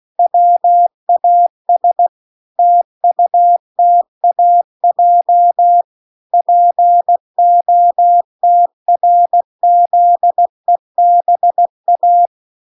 Morse code, heard as a keyed tone, about 16 wpm, 700 Hz. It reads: WAS TUTAJ POTRZEBA